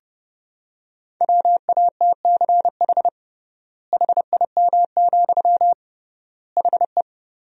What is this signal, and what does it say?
Morse code, heard as a keyed tone, about 30 wpm, 700 Hz. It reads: WATCH HIM, HE
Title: 3x Word-Spacing - Sentences from Top 300 Words - 30wpm